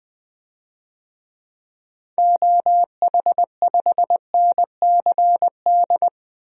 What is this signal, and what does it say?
Morse code, heard as a keyed tone, about 20 wpm, 700 Hz. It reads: OH5NCD
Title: DXCC-100 Call Signs - 6 Characters - Repeated 3x - 20wpm